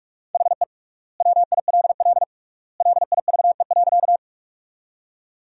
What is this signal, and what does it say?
Morse code, heard as a keyed tone, about 46 wpm, 700 Hz. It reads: HE WILL LIVE.
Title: 3x Word-Spacing - Sentences from Top 200 Words - 45wpm